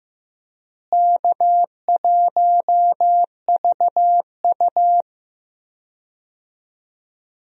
Morse code, heard as a keyed tone, about 15 wpm, 700 Hz. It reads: K1VU
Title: USA Call Signs - 4 Characters - Repeated 3x - 15wpm